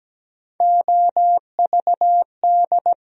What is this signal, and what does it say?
Morse code, heard as a keyed tone, about 17 wpm, 700 Hz. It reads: OVD
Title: USA Call Sign Postfixes - 3 Letters 17wpm